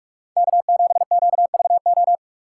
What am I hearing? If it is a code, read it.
Morse code, heard as a keyed tone, about 55 wpm, 700 Hz. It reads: K7QVO
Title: Speed Racing - USA 1x3 Call Signs 45wpm